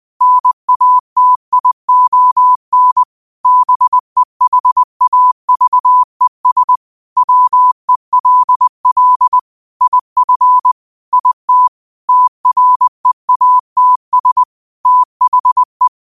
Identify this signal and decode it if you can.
Morse code, heard as a keyed tone, about 20 wpm, 1000 Hz. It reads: NATION BEHAVES WELL IF IT TREATS THE
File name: qotdfiles / 20wpm